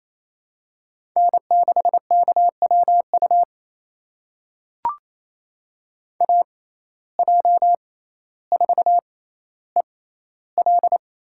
Morse code, heard as a keyed tone, about 29 wpm, spaced as for 15 wpm, 700 Hz. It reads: N6KWU AJ4EL
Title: USA Call Signs - 5 Characters - 6x Character Spacing - 28wpm